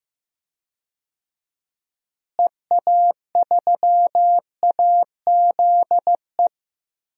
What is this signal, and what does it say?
Morse code, heard as a keyed tone, about 15 wpm, 700 Hz. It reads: EA3AZE
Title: DXCC-100 Call Signs - 6 Characters - Repeated 3x - 15wpm